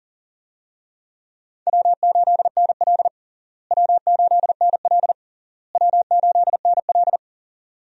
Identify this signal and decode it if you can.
Morse code, heard as a keyed tone, about 40 wpm, 700 Hz. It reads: W8NL W8NL W8NL
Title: USA Call Signs - 4 Characters - Repeated 3x - 40wpm